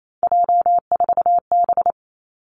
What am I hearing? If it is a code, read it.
Morse code, heard as a keyed tone, about 28 wpm, 700 Hz. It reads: J4B
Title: Three Challenging Letter-Number Combinations 28wpm